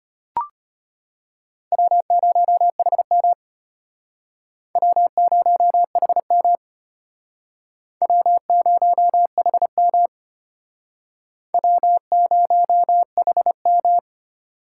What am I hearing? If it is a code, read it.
Morse code, heard as a keyed tone, about 34 wpm, 700 Hz. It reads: W0HM W0HM W0HM W0HM